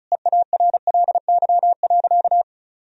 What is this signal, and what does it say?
Morse code, heard as a keyed tone, about 34 wpm, 700 Hz. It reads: EARLY.
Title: Sentences from Top 1000 Words - Review Part 2 - 35wpm